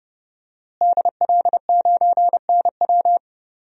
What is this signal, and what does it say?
Morse code, heard as a keyed tone, about 29 wpm, 700 Hz. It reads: DL9NW
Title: DXCC-100 Call Signs - 5 Characters - Repeated 2x - 30wpm